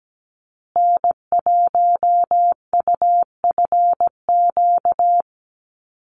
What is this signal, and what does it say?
Morse code, heard as a keyed tone, about 17 wpm, 700 Hz. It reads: N1UFQ
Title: USA Call Signs Repeated 3x 17wpm